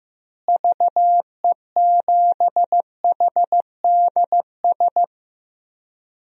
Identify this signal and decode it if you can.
Morse code, heard as a keyed tone, about 15 wpm, 700 Hz. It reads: VE7HDS
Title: DXCC-100 Call Signs - 6 Characters - Repeated 2x - 15wpm